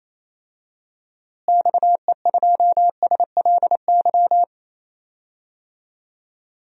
Morse code, heard as a keyed tone, about 28 wpm, 700 Hz. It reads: XE2SLY
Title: DXCC-100 Call Signs - 6 Characters - Repeated 3x - 28wpm